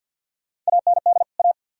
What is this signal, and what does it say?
Morse code, heard as a keyed tone, about 70 wpm, spaced as for 60 wpm, 700 Hz. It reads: AND A
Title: Speed-Racing - Sets of 2 Words 50wpm